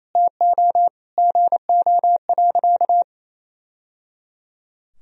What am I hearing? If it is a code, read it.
Morse code, heard as a keyed tone, about 28 wpm, 700 Hz. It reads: TO GO.
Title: Sentences from Top 1000 Words - Review Part 4 - 28wpm